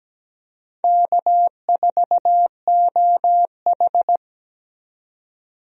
Morse code, heard as a keyed tone, about 17 wpm, 700 Hz. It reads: K4OH
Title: USA Call Signs - 4 Characters - 17wpm